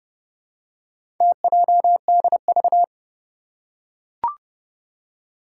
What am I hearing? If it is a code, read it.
Morse code, heard as a keyed tone, about 30 wpm, 700 Hz. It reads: TJDV
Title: Four Challenging Letter-Number Combinations 30wpm